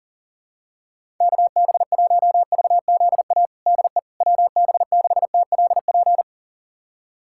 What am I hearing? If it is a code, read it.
Morse code, heard as a keyed tone, about 40 wpm, 700 Hz. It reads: KB1VZA DE WB6TLP
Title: USA Callsign DE Callsign - 6 Characters - 40wpm